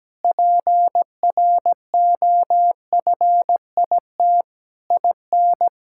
Morse code, heard as a keyed tone, about 17 wpm, 700 Hz. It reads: PROFIT IN
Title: Sentences from Top 1000 Words - Review Part 5 - 17wpm